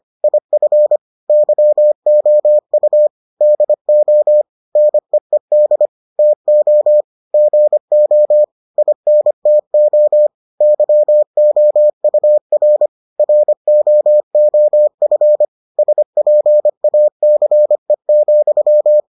Morse code, heard as a keyed tone, about 25 wpm, 600 Hz. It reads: IF YOU DO NEED TO GO INTO YOUR ROOF SPACE,